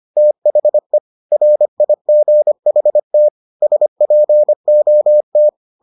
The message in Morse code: THE RIGHT SPOT

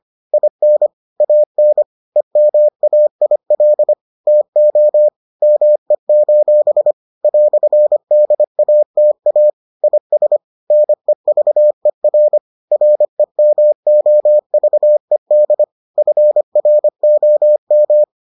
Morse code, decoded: IN AN EMAIL TO ME: "DATA IS NEVER REMOVED FROM

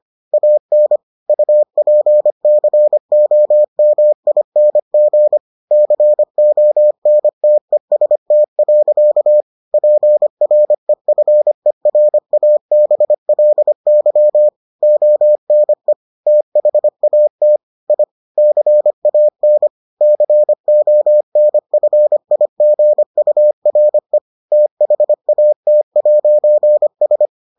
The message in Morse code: AN UPCOMING CONTEST. PREFERABLY ONE THAT I CAN CONFIGURE THAT'S

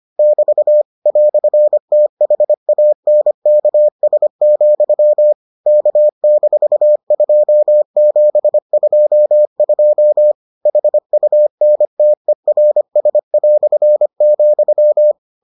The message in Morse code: <BT> "THANKS, K-2722 HUNTERS",

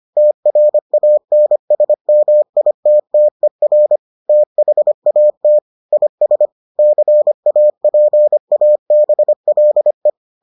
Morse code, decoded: TRANSMITTER THAT IS CAPABLE

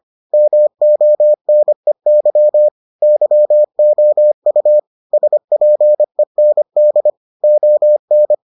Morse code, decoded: MONEY YOU SPEND ON

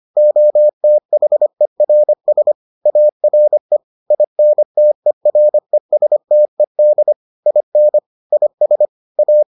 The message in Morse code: OTHERS ARE INTERESTED IN IS A

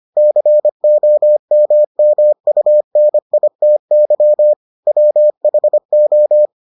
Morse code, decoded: COMMUNITY WHO

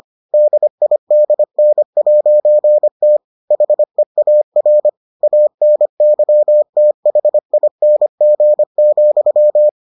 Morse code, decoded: DIDN'T HEAR ANYTHING,

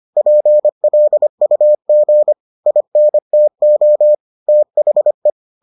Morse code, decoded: PLUG INTO THE

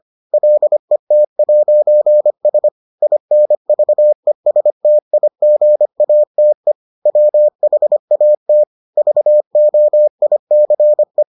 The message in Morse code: LET'S INVESTIGATE WHAT VOICE

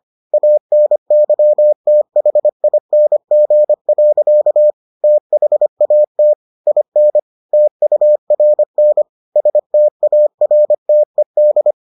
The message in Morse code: ANYTHING. THAT IN TURN STARTED

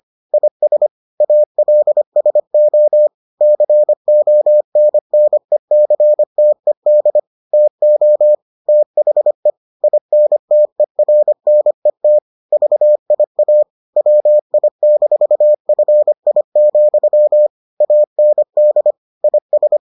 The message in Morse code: IS ALSO CONNECTED TO THE INTERNET VIA WI-FI, AND IS

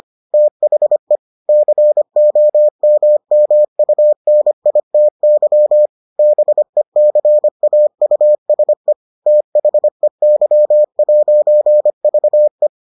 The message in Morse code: THE COMMUNITY BECAUSE THEY'VE

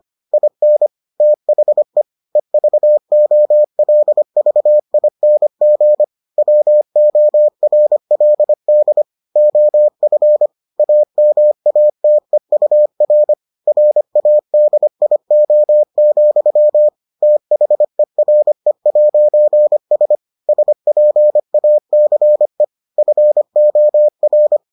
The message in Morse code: IN THE EVOLVING WORLD OF AMATEUR RADIO, THERE'S SPACE FOR

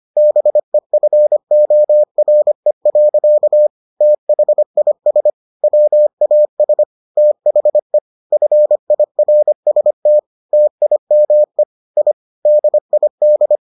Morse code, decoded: BEFORE. THIS WAS THE FIRST TIME I DID

